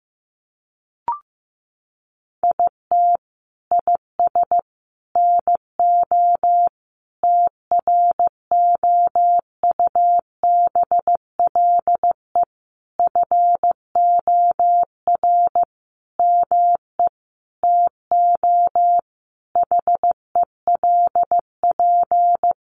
Morse code, heard as a keyed tone, about 15 wpm, 700 Hz. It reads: IT IS NO TROUBLE FOR ME TO HELP
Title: Sentences from Top 1200 Words 15wpm